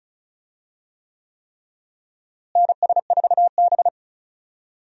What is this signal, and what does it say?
Morse code, heard as a keyed tone, about 36 wpm, 700 Hz. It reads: NS4B
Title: USA Call Signs - 4 Characters - Repeated 3x - 35wpm